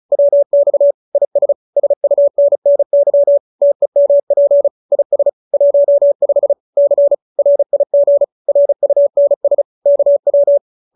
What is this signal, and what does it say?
Morse code, heard as a keyed tone, about 34 wpm, 570 Hz. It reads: WX IS SUNNY TEMP IS 15 C RIG RUNS KW